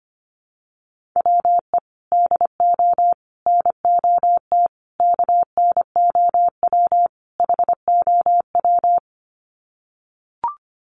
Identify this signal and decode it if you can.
Morse code, heard as a keyed tone, about 25 wpm, 700 Hz. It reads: WE DO NOT KNOW HOW
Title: Sets of 5 Words Encore 25wpm